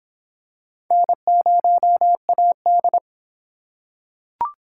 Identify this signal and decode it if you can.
Morse code, heard as a keyed tone, about 21 wpm, 700 Hz. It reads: N0AD
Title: Speed Racing - USA 1x2 Call Signs 17wpm